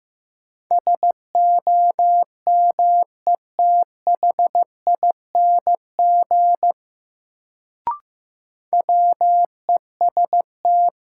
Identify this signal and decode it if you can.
Morse code, heard as a keyed tone, about 15 wpm, 700 Hz. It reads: SOMETHING WEST
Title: Sets of 5 Words - New Word Drill - 15wpm